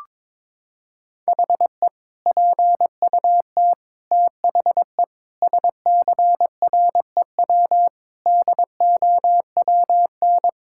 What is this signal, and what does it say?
Morse code, heard as a keyed tone, about 22 wpm, 700 Hz. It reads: HE PUT THE SCREW DOWN
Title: Sentences from Top 1600 Words 22wpm